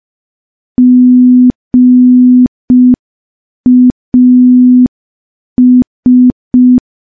Morse code, heard as a keyed tone, about 5 wpm, 250 Hz. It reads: GAS
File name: morse / morse-training-05WPM